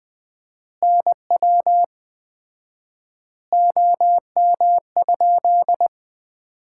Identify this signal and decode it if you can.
Morse code, heard as a keyed tone, about 20 wpm, 700 Hz. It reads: NW OM?
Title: Common QSO Phrases - 4x Word-Spacing - 20wpm